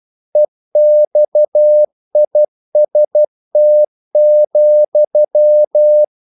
Morse code, heard as a keyed tone, about 12 wpm, 600 Hz. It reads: EXIST,
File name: CW / Practice / EnglishQuotations0312